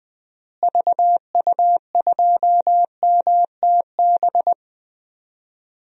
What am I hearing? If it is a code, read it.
Morse code, heard as a keyed tone, about 20 wpm, 700 Hz. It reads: VU2MTB
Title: DXCC-100 Call Signs - 6 Characters - Repeated 3x - 20wpm